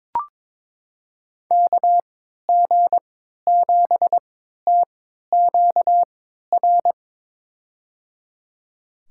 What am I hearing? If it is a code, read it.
Morse code, heard as a keyed tone, about 22 wpm, spaced as for 13 wpm, 700 Hz. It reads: KG7TQR
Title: USA Call Signs - 6 Characters - 3x Character Spacing - 22wpm